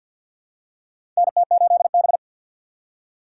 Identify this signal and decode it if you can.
Morse code, heard as a keyed tone, about 51 wpm, 700 Hz. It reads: NT8B
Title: USA Call Signs - 4 Characters - 3x Character Spacing - 50wpm